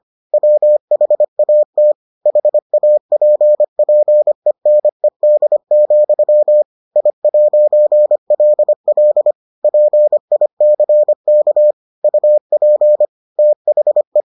Morse code, heard as a keyed tone, about 25 wpm, 600 Hz. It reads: WHAT HAPPENED, I'LL PICK UP THE